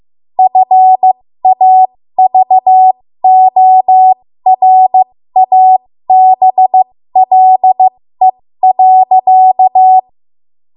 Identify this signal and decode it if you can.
Morse code, heard as a keyed tone, about 15 wpm, spaced as for 13 wpm, 750 Hz. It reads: FAVORABLE.
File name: Morse / 13WPM